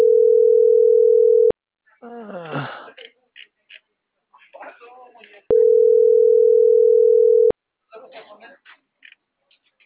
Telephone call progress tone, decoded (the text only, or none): ringback tone